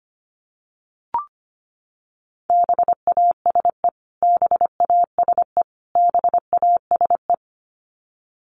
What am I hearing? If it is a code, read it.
Morse code, heard as a keyed tone, about 25 wpm, 700 Hz. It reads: BASE BASE BASE
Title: Top 300 Words Repeated 25wpm